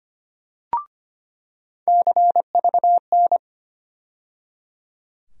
Morse code, heard as a keyed tone, about 25 wpm, 700 Hz. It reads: CVN
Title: USA Call Sign Postfixes - 3 Letters 25wpm